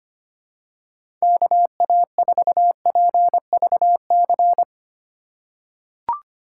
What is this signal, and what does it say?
Morse code, heard as a keyed tone, about 25 wpm, 700 Hz. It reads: KA4PVC
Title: USA Call Signs - 6 Characters - 3x Character Spacing - 25wpm